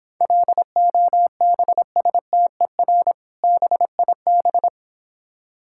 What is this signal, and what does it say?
Morse code, heard as a keyed tone, about 26 wpm, 700 Hz. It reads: LOBSTER BIB